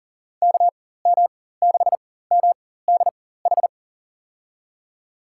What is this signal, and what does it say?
Morse code, heard as a keyed tone, about 40 wpm, spaced as for 19 wpm, 700 Hz. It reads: KM6MDH
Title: USA Call Signs - 6 Characters - 4x Character Spacing - 40wpm